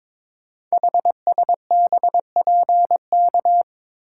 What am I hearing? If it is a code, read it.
Morse code, heard as a keyed tone, about 22 wpm, 700 Hz. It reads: HSBPK
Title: Code Groups - 5 Letters - 22wpm